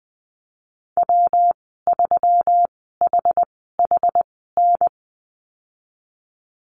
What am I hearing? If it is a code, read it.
Morse code, heard as a keyed tone, about 20 wpm, spaced as for 15 wpm, 700 Hz. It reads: W3HHN
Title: 147 Lesson 24g - Letter y - Call Signs 20wpm